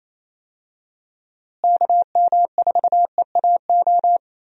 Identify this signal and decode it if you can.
Morse code, heard as a keyed tone, about 27 wpm, 700 Hz. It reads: KM4EAO
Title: USA Call Signs - 6 Characters - Repeated 3x - 28wpm